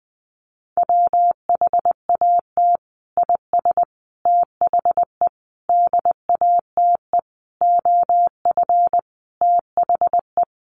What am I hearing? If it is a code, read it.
Morse code, heard as a keyed tone, about 20 wpm, 700 Hz. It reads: WHAT IS THE DATE OF THE
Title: Sentences from Top 1100 Words 20wpm